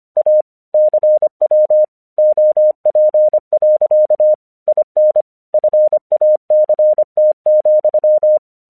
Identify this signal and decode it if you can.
Morse code, heard as a keyed tone, about 25 wpm, 610 Hz. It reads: A CW OP. IN FACT,
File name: 1 English 25wpm / AQC08